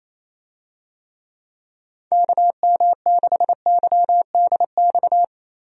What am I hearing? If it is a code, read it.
Morse code, heard as a keyed tone, about 28 wpm, 700 Hz. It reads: KM6YDX
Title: USA Call Signs - 6 Characters - Repeated 3x - 28wpm